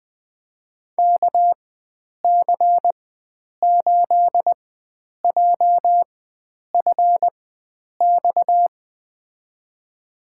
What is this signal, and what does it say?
Morse code, heard as a keyed tone, about 20 wpm, spaced as for 9 wpm, 700 Hz. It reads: KC8JFX
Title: USA Call Signs - 4x Character Spacing - 20wpm